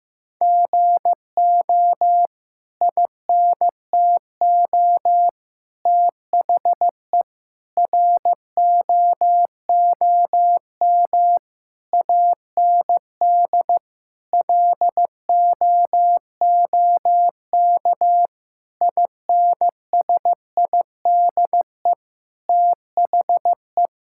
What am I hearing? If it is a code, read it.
Morse code, heard as a keyed tone, about 15 wpm, 700 Hz. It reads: GO INTO THE ROOM AND LOOK INSIDE THE
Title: Sentences from Top 1000 Words - Review Part 6 - 15wpm